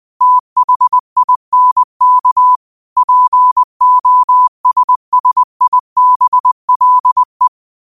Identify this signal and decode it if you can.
Morse code, heard as a keyed tone, about 20 wpm, 1000 Hz. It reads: THINK POSSIBLE
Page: Quotes for Fri, 15 Aug 2025 in Morse Code at 20 words per minute.